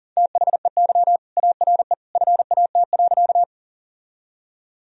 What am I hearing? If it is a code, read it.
Morse code, heard as a keyed tone, about 40 wpm, 700 Hz. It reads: THEY ARE FAT.
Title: Sentences from Top 1000 Words - Review Part 4 - 40wpm